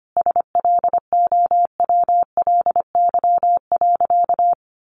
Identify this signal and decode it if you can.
Morse code, heard as a keyed tone, about 25 wpm, 700 Hz. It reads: SLOWLY.
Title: Sentences from Top 1100 Words 25wpm